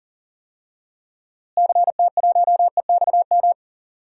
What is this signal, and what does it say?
Morse code, heard as a keyed tone, about 40 wpm, 700 Hz. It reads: CT1EXM
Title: DXCC-100 Call Signs - 6 Characters - Repeated 2x - 40wpm